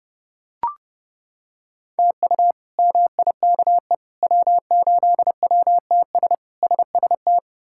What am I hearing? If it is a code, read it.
Morse code, heard as a keyed tone, about 30 wpm, 700 Hz. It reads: TU MIKE W8WTS SST
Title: K1USN SST QSO Practice - 30wpm